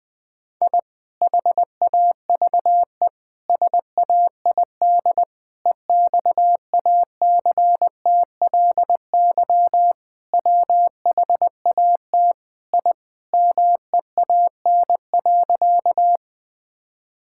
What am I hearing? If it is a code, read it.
Morse code, heard as a keyed tone, about 20 wpm, 700 Hz. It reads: I HAVE SAID EXACTLY WHAT I MEAN.
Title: Sentences from Top 1000 Words - Review Part 6 - 20wpm